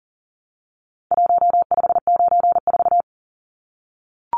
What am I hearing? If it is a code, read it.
Morse code, heard as a keyed tone, about 40 wpm, 700 Hz. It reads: 1594
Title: Numbers - Contest Style - 4-Digit 40wpm